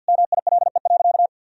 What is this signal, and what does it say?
Morse code, heard as a keyed tone, about 49 wpm, 700 Hz. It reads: MILE.
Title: Sentences from Top 1000 Words - Review Part 4 - 50wpm